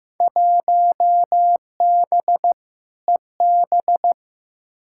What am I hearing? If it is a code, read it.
Morse code, heard as a keyed tone, about 15 wpm, 700 Hz. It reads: 1B EB
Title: Field Day Exchanges Encore - 15wpm